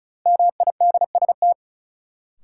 Morse code, heard as a keyed tone, about 34 wpm, 700 Hz. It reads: MIDST
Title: Sets of 4 Words - New Word Drill - 35wpm